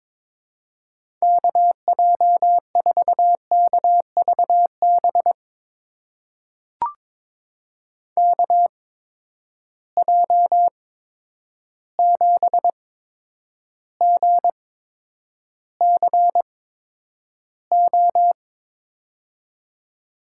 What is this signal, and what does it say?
Morse code, heard as a keyed tone, about 22 wpm, spaced as for 9 wpm, 700 Hz. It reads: KJ4KVB KJ7GCO
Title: USA Call Signs - 6 Characters - 8x Character Spacing - 22wpm